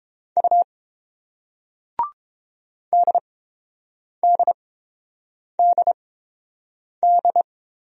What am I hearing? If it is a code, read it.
Morse code, heard as a keyed tone, about 30 wpm, spaced as for 7 wpm, 700 Hz. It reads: U DDDD